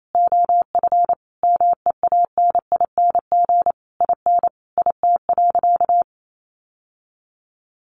Morse code, heard as a keyed tone, about 28 wpm, 700 Hz. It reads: OF MEANING IN IT.
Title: Sentences from Top 1200 Words 28wpm